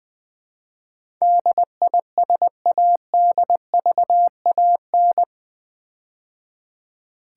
Morse code, heard as a keyed tone, about 20 wpm, 700 Hz. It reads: DISADVAN
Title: Build A Word - 4 Letters At A Time - 12-Letter Words - 20wpm